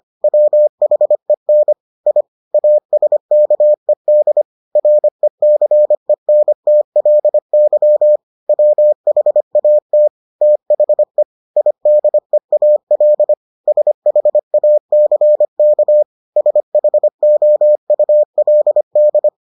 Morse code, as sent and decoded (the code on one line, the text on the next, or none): .-- .... -. / .. / .- ... -.- . -.. / .-. -.-. -. - .-.. -.-- / .-- .... .- - / - .... / .. -.. .- .-.. / ... .... .- -.-. -.- / ... .... --- ..- .-.. -..
WHEN I ASKED RECENTLY WHAT THE IDEAL SHACK SHOULD